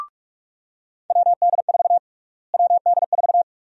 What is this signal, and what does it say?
Morse code, heard as a keyed tone, about 46 wpm, 700 Hz. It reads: WD4 WD4
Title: USA Call Sign Prefixes - Repeated 2x - 45wpm